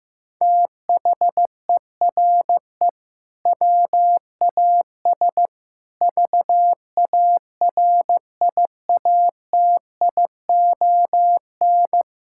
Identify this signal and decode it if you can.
Morse code, heard as a keyed tone, about 15 wpm, 700 Hz. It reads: THERE WAS VARIATION
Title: Sentences from Top 1600 Words 15wpm